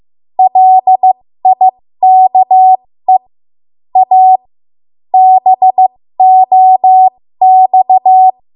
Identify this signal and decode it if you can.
Morse code, heard as a keyed tone, about 15 wpm, spaced as for 13 wpm, 750 Hz. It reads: LIKE A BOX
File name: Morse / 13WPM